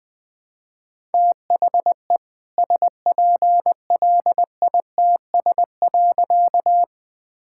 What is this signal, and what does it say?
Morse code, heard as a keyed tone, about 20 wpm, 700 Hz. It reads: THE SPLITS.
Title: All-Age Appropriate Jokes - 20wpm